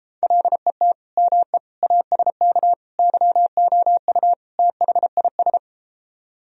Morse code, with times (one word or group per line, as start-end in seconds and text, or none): 0.2-0.9 LET
1.2-1.6 ME
1.8-2.7 ASK
3.0-4.3 YOU
4.6-5.6 THIS